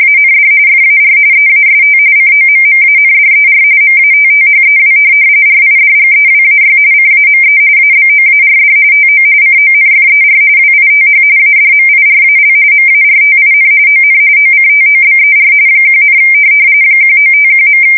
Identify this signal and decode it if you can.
RTTY, about 45.45 baud. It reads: AND FAR EAST COMMANDER MATTHEW RIDGEWAY CAME ABOARD TO CONFER WITH ADMIRAL MARTIN ON OCTOBER 1, 1951.
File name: fsk170 / USS-New-Jersey